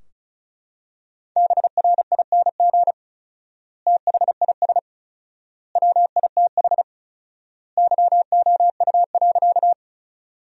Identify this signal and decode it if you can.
Morse code, heard as a keyed tone, about 34 wpm, 700 Hz. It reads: BRING THIS WITH YOU.
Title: 4x Word-Spacing - Sentences from Top 500 Words - 35wpm